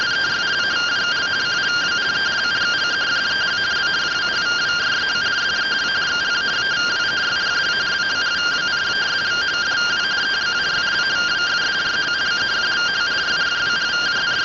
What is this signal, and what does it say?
Two-tone teleprinter signal, about 170 Hz shift.